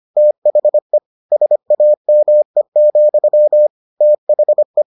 Morse code, heard as a keyed tone, about 25 wpm, 600 Hz. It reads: THE SAME, THE